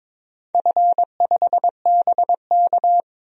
Morse code, heard as a keyed tone, about 22 wpm, 700 Hz. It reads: F5BK
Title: DXCC-100 Call Signs - 4 Characters - 22wpm